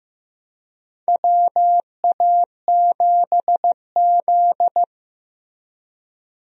Morse code, heard as a keyed tone, about 15 wpm, 700 Hz. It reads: WA7Z
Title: USA Call Signs - 4 Characters - Repeated 2x - 15wpm